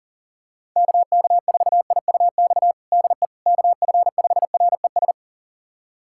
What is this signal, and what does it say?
Morse code, heard as a keyed tone, about 40 wpm, 700 Hz. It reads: KK4IUX DE KF5RES
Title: USA Callsign DE Callsign 40wpm